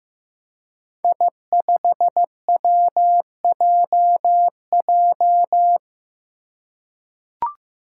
Morse code, heard as a keyed tone, about 15 wpm, 700 Hz. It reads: I5WJJ
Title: Five Challenging Letter-Number Combinations 15wpm